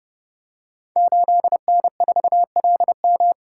Morse code, heard as a keyed tone, about 30 wpm, 700 Hz. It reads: 8N4LM
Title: Code Groups - 5 Letter:Numbers - 30wpm